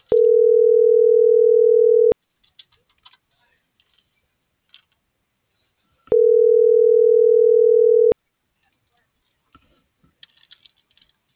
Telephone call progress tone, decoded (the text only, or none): ringback tone